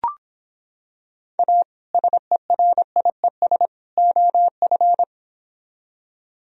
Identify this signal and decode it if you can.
Morse code, heard as a keyed tone, about 25 wpm, 700 Hz. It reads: A SERIES OF